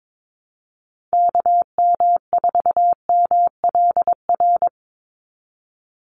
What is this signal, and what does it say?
Morse code, heard as a keyed tone, about 22 wpm, 700 Hz. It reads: KM4MLR
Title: USA Call Signs 22wpm